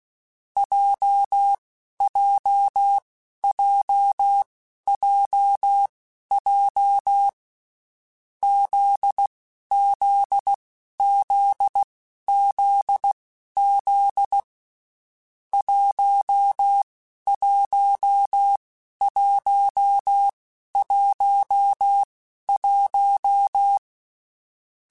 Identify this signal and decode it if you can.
Morse code, heard as a keyed tone, about 16 wpm, spaced as for 11 wpm, 800 Hz. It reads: JJJJJ ZZZZZ 11111
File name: Basic Sets / Basic Letter Groups Collective7 / JZ17@10